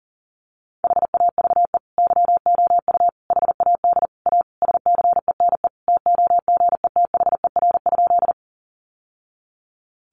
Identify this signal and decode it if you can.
Morse code, heard as a keyed tone, about 40 wpm, 700 Hz. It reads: HAVE YOU HAD A SCENE TOGETHER?
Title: Sentences from Top 1100 Words 40wpm